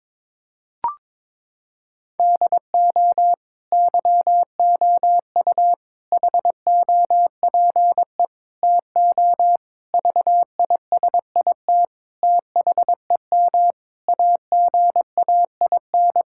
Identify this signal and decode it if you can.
Morse code, heard as a keyed tone, about 22 wpm, 700 Hz. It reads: DO YOU HOPE TO VISIT THEM AGAIN
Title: Sentences from Top 1000 Words - Review Part 6 - 22wpm